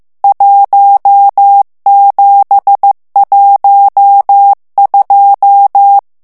Morse code, heard as a keyed tone, about 15 wpm, 800 Hz. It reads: 1712
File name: morse mp3 / ITALIEN texte clair 1 vitesse 14 mots minute